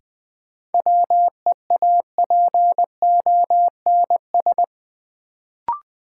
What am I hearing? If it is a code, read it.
Morse code, heard as a keyed tone, about 20 wpm, 700 Hz. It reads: WEAPONS
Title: Sets of 5 Words Encore - New Word Drill - 20wpm